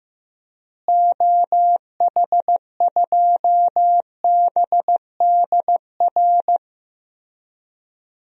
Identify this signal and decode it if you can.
Morse code, heard as a keyed tone, about 15 wpm, 700 Hz. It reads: OH2BDR
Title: DXCC-100 Call Signs - 6 Characters - Repeated 2x - 15wpm